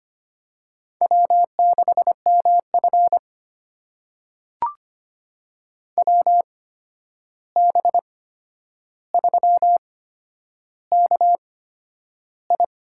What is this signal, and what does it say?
Morse code, heard as a keyed tone, about 25 wpm, spaced as for 10 wpm, 700 Hz. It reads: W6MF WB3KI
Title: USA Call Signs - 8x Character Spacing - 25wpm